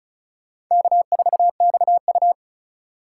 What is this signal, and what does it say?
Morse code, heard as a keyed tone, about 45 wpm, 700 Hz. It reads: K4XU